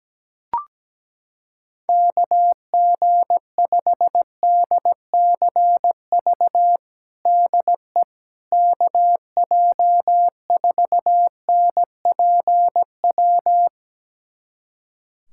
Morse code, heard as a keyed tone, about 17 wpm, 700 Hz. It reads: KG5DCV DE KJ4NPW
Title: USA Callsign DE Callsign - 6 Characters - 17wpm